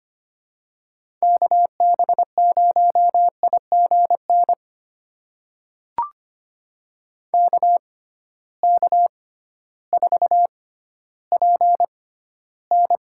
Morse code, heard as a keyed tone, about 25 wpm, spaced as for 13 wpm, 700 Hz. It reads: KB0IGN KK4PN
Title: USA Call Signs - 6 Characters - 6x Character Spacing - 25wpm